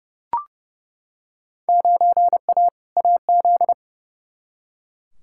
Morse code, heard as a keyed tone, about 32 wpm, 700 Hz. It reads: 9A AZ
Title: Field Day Exchanges - 30wpm